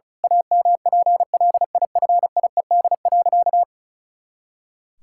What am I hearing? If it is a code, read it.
Morse code, heard as a keyed tone, about 36 wpm, 700 Hz. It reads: AMPLIFIED.
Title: Pangram Challenge v2 - 35wpm